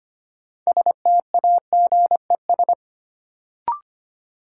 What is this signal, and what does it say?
Morse code, heard as a keyed tone, about 25 wpm, 700 Hz. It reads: STAGES
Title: Sets of 5 Words Encore - New Word Drill - 25wpm